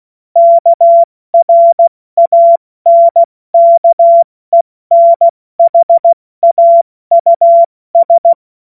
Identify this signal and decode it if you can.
Morse code, heard as a keyed tone, about 16 wpm, 680 Hz. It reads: KRANKENHAUS
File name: Witz-09-September-2025-70-F80